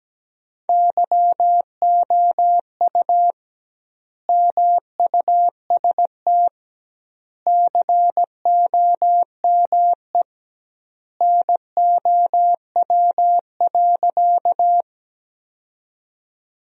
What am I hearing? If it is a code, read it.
Morse code, heard as a keyed tone, about 17 wpm, 700 Hz. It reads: YOU MUST COME NOW.
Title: 2x Word-Spacing - Sentences from Top 200 Words - 17wpm